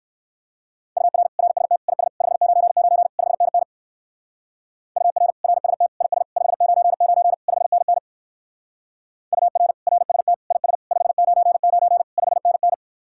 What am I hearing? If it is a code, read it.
Morse code, heard as a keyed tone, about 58 wpm, 700 Hz. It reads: UR RST IS 599 5NN UR RST IS 599 5NN UR RST IS 599 5NN